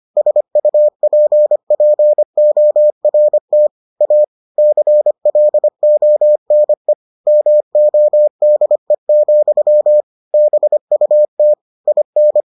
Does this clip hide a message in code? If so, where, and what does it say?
Yes: Morse code from 0.2 to 12.4 s, heard as a keyed tone, about 25 wpm, 600 Hz. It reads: SUPPORT A CLONE MODE, BUT IN